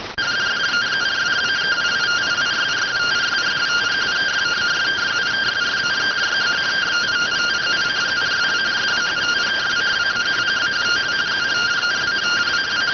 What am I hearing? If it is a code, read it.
Two-tone teleprinter signal, about 170 Hz shift.